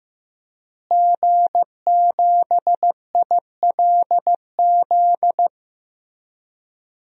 Morse code, heard as a keyed tone, about 15 wpm, 700 Hz. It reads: G7ILZ
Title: DXCC-100 Call Signs - 5 Characters - Repeated 3x - 15wpm